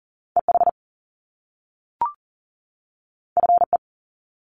Morse code, heard as a keyed tone, about 40 wpm, 700 Hz. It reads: EH FE
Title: Top Two Letter Combinations in Words 40wpm